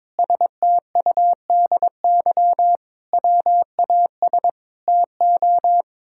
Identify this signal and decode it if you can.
Morse code, heard as a keyed tone, about 22 wpm, 700 Hz. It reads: STUDY WAS TO
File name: practice / 22wpm / Sets of 7 Words - 22wpm